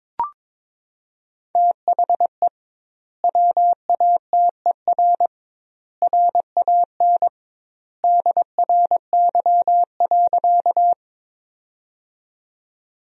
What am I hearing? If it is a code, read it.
Morse code, heard as a keyed tone, about 22 wpm, 700 Hz. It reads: THE WATER RAN DRY.
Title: 2x Word-Spacing - Sentences from Top 500 Words - 22wpm